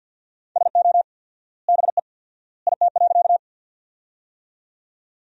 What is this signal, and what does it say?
Morse code, heard as a keyed tone, about 50 wpm, 700 Hz. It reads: SO BE IT.
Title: 4x Word-Spacing - Sentences from Top 100 Words - 50wpm